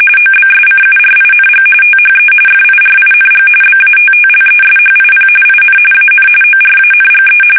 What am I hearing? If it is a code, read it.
RTTY, about 45.45 baud. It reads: OBSCURING THE FIRING BATTLESHIP IN AN ORANGE
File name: fsk850 / battleshipnj